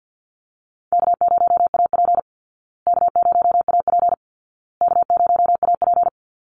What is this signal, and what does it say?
Morse code, heard as a keyed tone, about 50 wpm, 700 Hz. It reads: K0AP K0AP K0AP
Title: Call Signs Repeated 2x - 50wpm